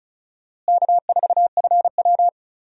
Morse code, heard as a keyed tone, about 34 wpm, 700 Hz. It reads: K4FW
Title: USA Call Signs - 4 Characters - Repeated 2x - 35wpm